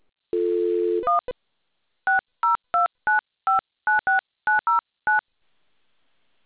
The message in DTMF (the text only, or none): 1 6029596909